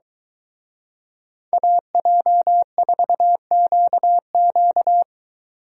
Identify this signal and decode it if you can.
Morse code, heard as a keyed tone, about 23 wpm, 700 Hz. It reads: AJ4QQ